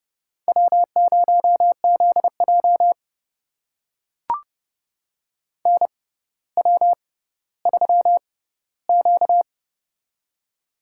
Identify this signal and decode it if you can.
Morse code, heard as a keyed tone, about 29 wpm, spaced as for 15 wpm, 700 Hz. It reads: W0ZJ NW3Q
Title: USA Call Signs - 4 Characters - 6x Character Spacing - 30wpm